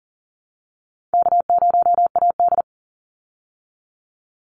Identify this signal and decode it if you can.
Morse code, heard as a keyed tone, about 40 wpm, 700 Hz. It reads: K0AD
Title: Easy Call Signs Repeated 2x - 40wpm